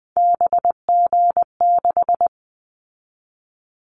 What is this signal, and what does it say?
Morse code, heard as a keyed tone, about 20 wpm, 700 Hz. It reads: BG6
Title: Callsign Prefixes for Top 100 DXCC - 20wpm